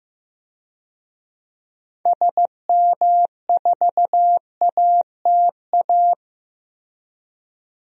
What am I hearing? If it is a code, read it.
Morse code, heard as a keyed tone, about 15 wpm, 700 Hz. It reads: SM4ATA
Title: DXCC-100 Call Signs - 6 Characters - 15wpm